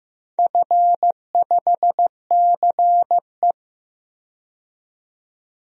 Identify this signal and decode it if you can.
Morse code, heard as a keyed tone, about 15 wpm, 700 Hz. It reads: F5CE
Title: DXCC-100 Call Signs - 4 Characters - 15wpm